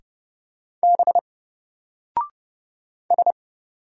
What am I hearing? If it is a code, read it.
Morse code, heard as a keyed tone, about 30 wpm, 700 Hz. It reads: B S
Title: Single Character (Letter:Number:Punctuation:Prosign) 30wpm